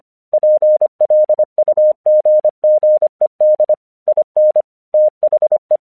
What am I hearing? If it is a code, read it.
Morse code, heard as a keyed tone, about 24 wpm, 610 Hz. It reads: PLUGGED IN THE
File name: 1 English 25wpm / AQC27